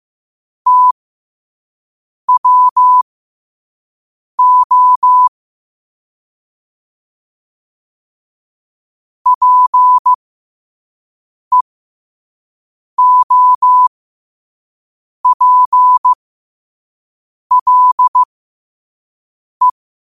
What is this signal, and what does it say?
Morse code, heard as a keyed tone, about 15 wpm, spaced as for 5 wpm, 1000 Hz. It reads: TWO PEOPLE
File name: qotdfiles / 5wpm